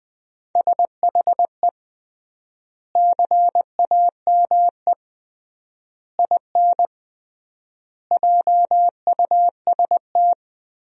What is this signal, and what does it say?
Morse code, heard as a keyed tone, about 20 wpm, 700 Hz. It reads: SHE CAME IN JUST
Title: 3x Word-Spacing - Sentences from Top 300 Words - 20wpm